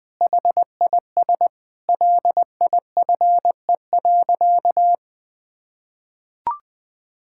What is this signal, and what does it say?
Morse code, heard as a keyed tone, about 20 wpm, 700 Hz. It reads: HIS LIFE.
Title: Sentences with 5-Word Phrases - 20wpm